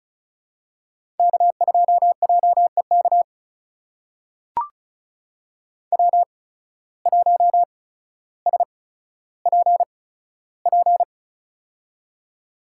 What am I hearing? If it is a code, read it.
Morse code, heard as a keyed tone, about 36 wpm, spaced as for 15 wpm, 700 Hz. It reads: K2JEK W1SPP
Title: USA Call Signs - 5 Characters - 8x Character Spacing - 35wpm